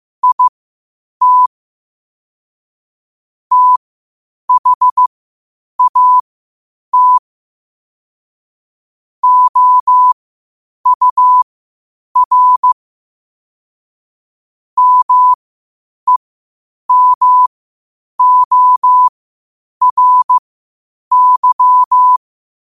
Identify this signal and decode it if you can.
Morse code, heard as a keyed tone, about 15 wpm, spaced as for 8 wpm, 1000 Hz. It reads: IT THAT OUR MEMORY
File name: qotdfiles / 8wpm